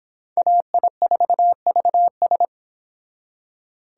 Morse code, heard as a keyed tone, about 25 wpm, 700 Hz. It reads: AI4VS